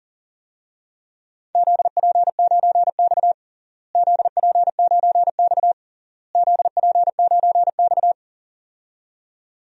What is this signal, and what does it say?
Morse code, heard as a keyed tone, about 40 wpm, 700 Hz. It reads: ZP9X ZP9X ZP9X
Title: DXCC-100 Call Signs - 4 Characters - Repeated 3x - 40wpm